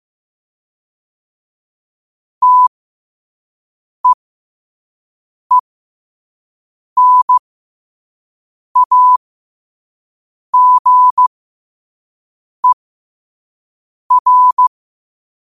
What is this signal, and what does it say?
Morse code, heard as a keyed tone, about 15 wpm, spaced as for 5 wpm, 1000 Hz. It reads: TEENAGER
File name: qotdfiles / 5wpm